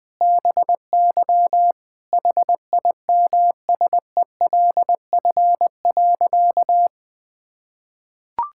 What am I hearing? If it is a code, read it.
Morse code, heard as a keyed tone, about 20 wpm, 700 Hz. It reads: BY HIMSELF.
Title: Sentences from Top 1600 Words 20wpm